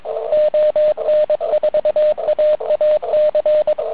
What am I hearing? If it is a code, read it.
Morse code, heard as a keyed tone, about 23 wpm, 590 Hz. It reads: ON4AAC